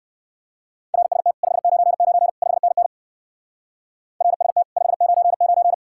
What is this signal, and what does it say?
Morse code, heard as a keyed tone, about 55 wpm, 700 Hz. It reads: RST 599 5NN RST 599
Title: Speed-Racing Common QSO Phrases - 45wpm